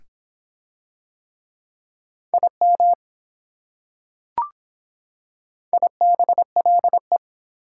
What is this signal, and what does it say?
Morse code, heard as a keyed tone, about 24 wpm, 700 Hz. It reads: IM IBLE